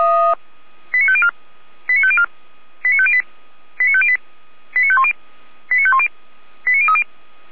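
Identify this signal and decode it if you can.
Five-tone selective call (ZVEI), heard as quick runs of five tones, tones of about 70 ms. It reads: 89504 89504 89508 89508 89510 89510 89030
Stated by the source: ZVEI five-tone call 89001 received by SDR, archived by Feuerwehr Orlamünde fire brigade.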